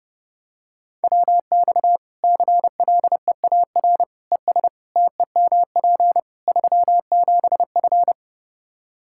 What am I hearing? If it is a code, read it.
Morse code, heard as a keyed tone, about 30 wpm, 700 Hz. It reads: WX CLEAR ES TEMP 37F
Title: K1USN SST QSO Practice 30wpm